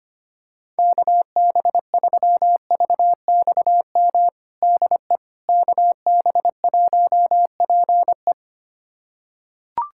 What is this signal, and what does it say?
Morse code, heard as a keyed tone, about 25 wpm, 700 Hz. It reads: KB3VXM DE KB1PE
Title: USA Callsign DE Callsign 25wpm